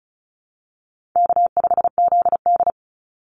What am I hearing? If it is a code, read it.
Morse code, heard as a keyed tone, about 34 wpm, 700 Hz. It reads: K5ZD
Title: Easy Call Signs Repeated 2x - 35wpm